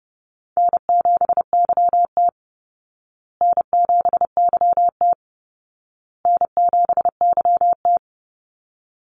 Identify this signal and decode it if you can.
Morse code, heard as a keyed tone, about 28 wpm, 700 Hz. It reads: N7YT N7YT N7YT
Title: Easy Call Signs Repeated 3x - 30wpm